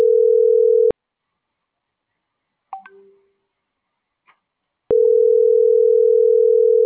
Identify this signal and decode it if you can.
Ringback tone (North America), 440 + 480 Hz.